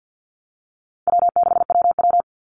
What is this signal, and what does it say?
Morse code, heard as a keyed tone, about 50 wpm, 700 Hz. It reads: W6WW
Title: NA SOTA Chaser Callsigns 50wpm